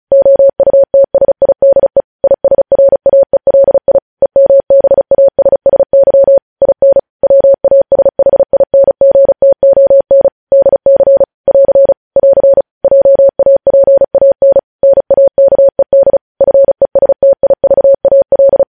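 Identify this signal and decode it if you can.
Morse code, heard as a keyed tone, about 36 wpm, 550 Hz. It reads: OUTSIDE ISRAELI EMBASSY IN WASHINGTON DC <AR> <AR> JAPAN NAKED FESTIVAL
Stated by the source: Morse code news headlines practice sent at 35 wpm.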